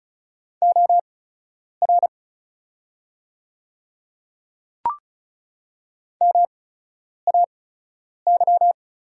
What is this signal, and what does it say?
Morse code, heard as a keyed tone, about 36 wpm, spaced as for 10 wpm, 700 Hz. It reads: OR MAY
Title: 8x Character Spacing - Top 100 Words - 35wpm